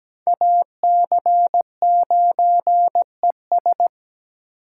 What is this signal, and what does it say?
Morse code, heard as a keyed tone, about 17 wpm, 700 Hz. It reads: AC9ES
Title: USA Call Signs - 5 Characters - 6x Character Spacing - 17wpm